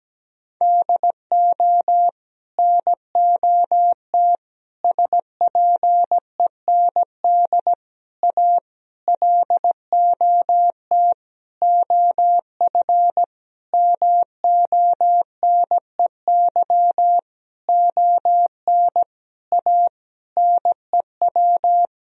Morse code, heard as a keyed tone, about 17 wpm, 700 Hz. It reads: DO NOT SPEND A LOT OF MONEY ON A NEW
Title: Sentences from Top 1000 Words - Review Part 4 - 17wpm